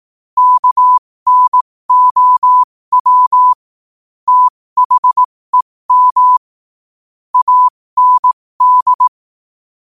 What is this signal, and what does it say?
Morse code, heard as a keyed tone, about 18 wpm, spaced as for 15 wpm, 1000 Hz. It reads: KNOW THEM AND
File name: qotdfiles / 15wpm